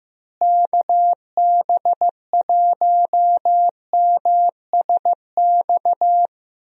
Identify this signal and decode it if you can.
Morse code, heard as a keyed tone, about 15 wpm, 700 Hz. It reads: KB1MSX
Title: USA Call Signs - 6 Characters - Repeated 3x - 15wpm